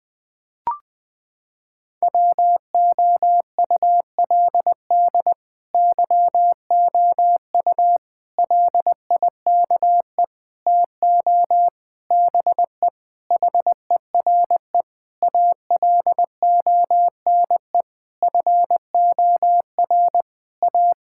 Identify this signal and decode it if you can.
Morse code, heard as a keyed tone, about 20 wpm, 700 Hz. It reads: WOULD YOU LIKE TO BE HERE ALONE FOR A
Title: Sentences from Top 1000 Words - Review Part 5 - 20wpm